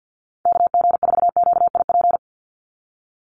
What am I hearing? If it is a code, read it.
Morse code, heard as a keyed tone, about 50 wpm, 700 Hz. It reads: KG4QIP
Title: USA Call Signs 50wpm